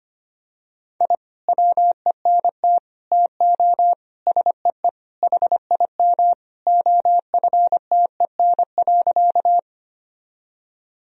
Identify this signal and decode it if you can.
Morse code, heard as a keyed tone, about 25 wpm, 700 Hz. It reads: I WENT TO SEE HIM OFTEN.
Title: Sentences from Top 1000 Words - Review Part 1 - 25wpm